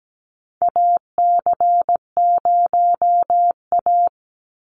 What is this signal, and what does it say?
Morse code, heard as a keyed tone, about 17 wpm, 700 Hz. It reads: AC0A
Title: NA SOTA Chaser Callsigns 17wpm